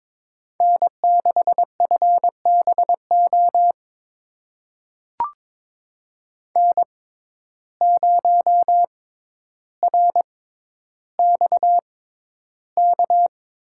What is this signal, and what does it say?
Morse code, heard as a keyed tone, about 22 wpm, spaced as for 11 wpm, 700 Hz. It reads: N6FBO N0RXK
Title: USA Call Signs - 5 Characters - 6x Character Spacing - 22wpm